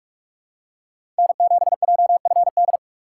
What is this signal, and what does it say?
Morse code, heard as a keyed tone, about 45 wpm, 700 Hz. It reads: N7JFD
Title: USA Call Signs - 5 Characters - 2x Character Spacing - 45wpm